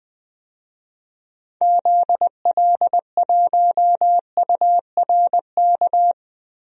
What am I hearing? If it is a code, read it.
Morse code, heard as a keyed tone, about 20 wpm, 700 Hz. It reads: ZL1URK
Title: DXCC-100 Call Signs - 6 Characters - Repeated 3x - 20wpm